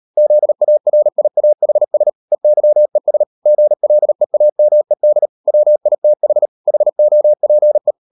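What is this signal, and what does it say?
Morse code, heard as a keyed tone, about 38 wpm, 600 Hz. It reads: ZARIAHS EYES GLEAMED WITH HOPE
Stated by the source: Morse code practice at 38 WPM, sending Ken Follett's novel Never in English, from CW med Gnister.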